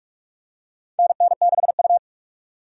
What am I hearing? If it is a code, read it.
Morse code, heard as a keyed tone, about 47 wpm, 700 Hz. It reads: NN6U
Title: USA Call Signs - 4 Characters - 6x Character Spacing - 45wpm